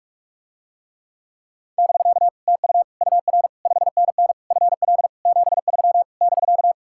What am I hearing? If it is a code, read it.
Morse code, heard as a keyed tone, about 49 wpm, 700 Hz. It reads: <BK> TU UR 5NN FL 73 <BK>